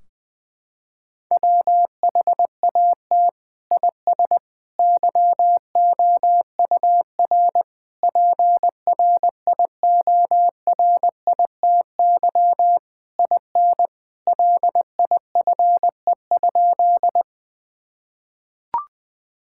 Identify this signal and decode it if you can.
Morse code, heard as a keyed tone, about 21 wpm, 700 Hz. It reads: WHAT IS YOUR PRIORITY IN LIFE?
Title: Sentences from Top 1600 Words 20wpm